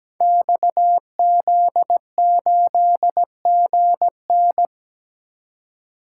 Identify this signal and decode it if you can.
Morse code, heard as a keyed tone, about 17 wpm, 700 Hz. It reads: XZ8GN
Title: Code Groups - 5 Letter:Numbers - 17wpm